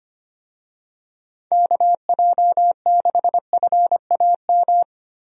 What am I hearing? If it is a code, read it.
Morse code, heard as a keyed tone, about 25 wpm, 700 Hz. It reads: KJ6FAM
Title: USA Call Signs - 6 Characters - Repeated 2x - 25wpm